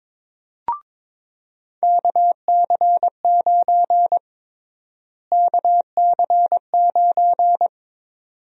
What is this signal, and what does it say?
Morse code, heard as a keyed tone, about 22 wpm, 700 Hz. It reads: KC9 KC9
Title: USA Call Sign Prefixes - Repeated 2x - 22wpm